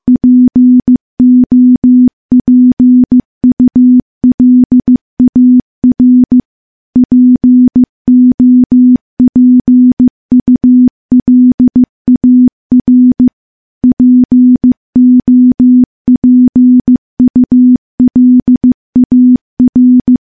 Morse code, heard as a keyed tone, about 15 wpm, 250 Hz. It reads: POPULAR POPULAR POPULAR
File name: morse-training-15WPM